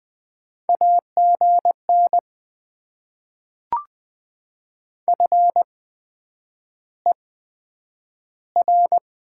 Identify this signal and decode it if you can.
Morse code, heard as a keyed tone, about 20 wpm, spaced as for 9 wpm, 700 Hz. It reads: AGN FER
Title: Top 20 QSO Elements - 8x Character Spacing - 20wpm